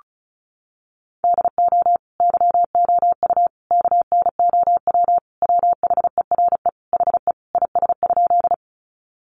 Morse code, heard as a keyed tone, about 34 wpm, 700 Hz. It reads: DO YOU KNOW WHERE HE IS?
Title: Sentences from Top 300 Words 35wpm